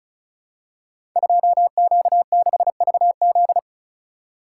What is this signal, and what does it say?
Morse code, heard as a keyed tone, about 35 wpm, 700 Hz. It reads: 2Q6VZ
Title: Code Groups - 5 Letter:Numbers - 35wpm